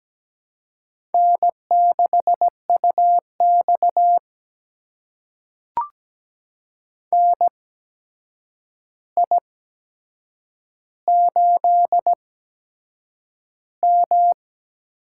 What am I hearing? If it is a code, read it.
Morse code, heard as a keyed tone, about 17 wpm, spaced as for 7 wpm, 700 Hz. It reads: N6UX NI8M
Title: USA Call Signs - 4 Characters - 8x Character Spacing - 17wpm